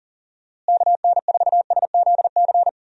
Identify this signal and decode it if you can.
Morse code, heard as a keyed tone, about 37 wpm, 700 Hz. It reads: KN4SZC
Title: USA Call Signs - 6 Characters - Repeated 2x - 40wpm